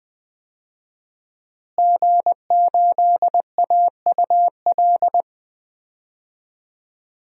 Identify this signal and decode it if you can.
Morse code, heard as a keyed tone, about 20 wpm, 700 Hz. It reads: G8AUL
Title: DXCC-100 Call Signs - 5 Characters - Repeated 2x - 20wpm